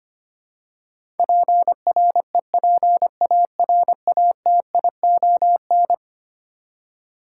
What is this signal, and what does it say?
Morse code, heard as a keyed tone, about 25 wpm, 700 Hz. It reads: PREPARATION
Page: morse-code-ninja-practice-25wpm